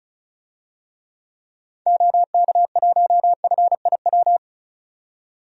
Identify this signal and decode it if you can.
Morse code, heard as a keyed tone, about 36 wpm, 700 Hz. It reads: OK1FIW
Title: DXCC-100 Call Signs - 6 Characters - 35wpm